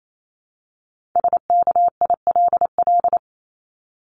Morse code, heard as a keyed tone, about 28 wpm, 700 Hz. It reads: SKILL
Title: Top 1001-1250 Words 28wpm